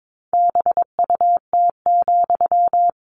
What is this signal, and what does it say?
Morse code, heard as a keyed tone, about 22 wpm, 700 Hz. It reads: BUT,
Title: Sentences from Top 1500 Words 22wpm